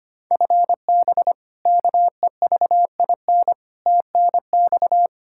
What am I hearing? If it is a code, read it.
Morse code, heard as a keyed tone, about 25 wpm, 700 Hz. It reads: FB KEVIN TNX
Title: Practice QSO - Exchange 3 - More Info - 25wpm